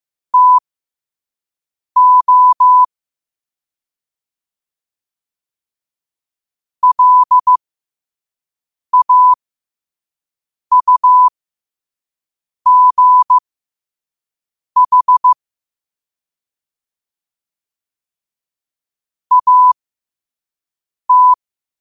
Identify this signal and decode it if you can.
Morse code, heard as a keyed tone, about 15 wpm, spaced as for 5 wpm, 1000 Hz. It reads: TO LAUGH AT
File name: qotdfiles / 5wpm